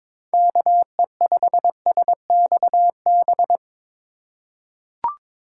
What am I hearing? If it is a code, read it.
Morse code, heard as a keyed tone, about 21 wpm, 700 Hz. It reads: KE5SXB
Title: USA Call Signs - 6 Characters - 2x Character Spacing - 22wpm